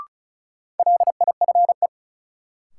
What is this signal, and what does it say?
Morse code, heard as a keyed tone, about 36 wpm, 700 Hz. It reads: LIFE
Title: Sets of 5 Words - New Word Drill - 35wpm